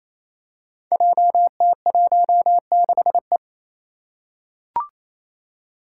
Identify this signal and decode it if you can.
Morse code, heard as a keyed tone, about 27 wpm, 700 Hz. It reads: JT16E
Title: Five Challenging Letter-Number Combinations 28wpm